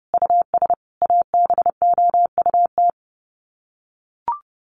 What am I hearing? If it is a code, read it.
Morse code, heard as a keyed tone, about 30 wpm, 700 Hz. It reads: US ABOUT
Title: Sets of 5 Words Encore2 30wpm